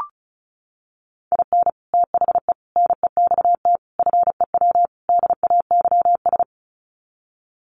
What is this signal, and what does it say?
Morse code, heard as a keyed tone, about 34 wpm, 700 Hz. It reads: IN THE NEXT FEW DAYS
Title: Sets of 5 Words Encore2 35wpm